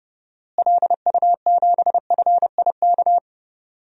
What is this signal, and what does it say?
Morse code, heard as a keyed tone, about 30 wpm, 700 Hz. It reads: LU7FIK
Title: DXCC-100 Call Signs - 6 Characters - Repeated 2x - 30wpm